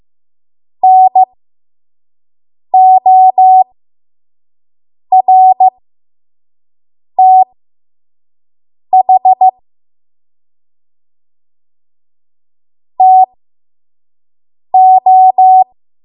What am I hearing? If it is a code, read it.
Morse code, heard as a keyed tone, about 15 wpm, spaced as for 5 wpm, 750 Hz. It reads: NORTH TO
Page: 5 WPM Code Archive